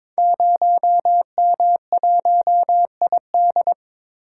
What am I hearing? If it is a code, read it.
Morse code, heard as a keyed tone, about 22 wpm, 700 Hz. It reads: 0M1ID
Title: Code Groups - 5 Letter:Numbers - 22wpm